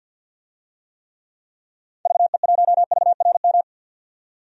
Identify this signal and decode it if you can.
Morse code, heard as a keyed tone, about 52 wpm, 700 Hz. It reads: VE1VRM
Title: DXCC-100 Call Signs - 6 Characters - 50wpm